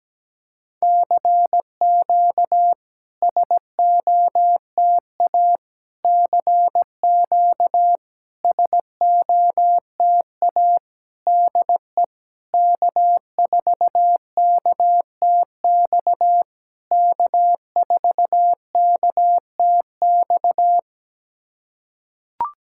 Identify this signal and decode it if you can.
Morse code, heard as a keyed tone, about 17 wpm, 700 Hz. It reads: CQ SOTA CQ SOTA DE K4KTX K4KTX
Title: SOTA QSO Practice 17wpm